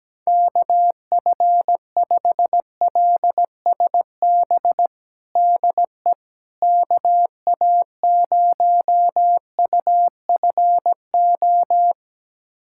Morse code, heard as a keyed tone, about 17 wpm, 700 Hz. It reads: KF5LSB DE KA0UFO
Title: USA Callsign DE Callsign 17wpm